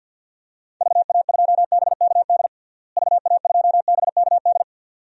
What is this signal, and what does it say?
Morse code, heard as a keyed tone, about 51 wpm, 700 Hz. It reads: VA2BKD VA2BKD
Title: DXCC-100 Call Signs - 6 Characters - Repeated 2x - 50wpm